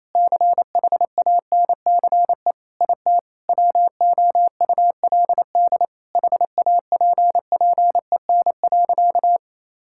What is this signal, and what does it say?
Morse code, heard as a keyed tone, about 27 wpm, 700 Hz. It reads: CHANCE IT WOULD HAPPEN.
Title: Sentences from Top 1000 Words - Review Part 6 - 28wpm